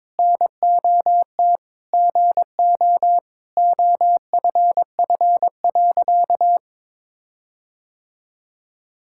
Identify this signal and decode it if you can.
Morse code, heard as a keyed tone, about 22 wpm, 700 Hz. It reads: NOT GO OFF.
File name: practice / 22wpm / Sentences with 4-Word Phrases - 22wpm